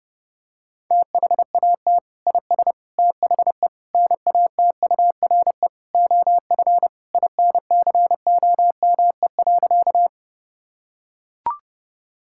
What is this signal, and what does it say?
Morse code, heard as a keyed tone, about 32 wpm, 700 Hz. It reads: THAT IS THE NATURE OF INCOME.
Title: Sentences from Top 1000 Words - Review Part 5 - 30wpm